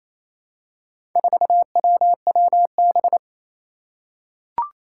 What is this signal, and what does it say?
Morse code, heard as a keyed tone, about 28 wpm, 700 Hz. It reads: 4WWB
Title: Four Challenging Letter-Number Combinations 28wpm